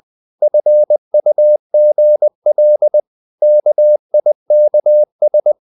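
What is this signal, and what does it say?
Morse code, heard as a keyed tone, about 20 wpm, 600 Hz. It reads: FUGL KIKS
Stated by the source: Morse code practice of short Danish words at 20 wpm, from CW med Gnister.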